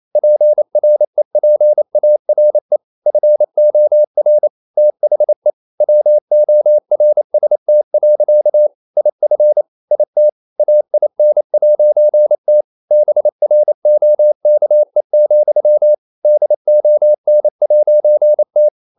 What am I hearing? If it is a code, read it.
Morse code, heard as a keyed tone, about 28 wpm, 600 Hz. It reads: PREPARE FOR THE WORST. IF IT AIN'T BROKE, DON'T
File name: CW / Practice / EnglishSayings28